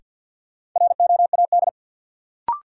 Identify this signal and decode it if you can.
Morse code, heard as a keyed tone, about 52 wpm, 700 Hz. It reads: ROAD
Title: Sentences from Top 400 Words - New Word Drill - 50wpm